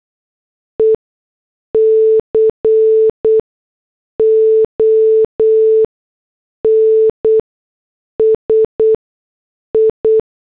Morse code, heard as a keyed tone, about 8 wpm, spaced as for 6 wpm, 440 Hz. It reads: ECONSI
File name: morse / headlines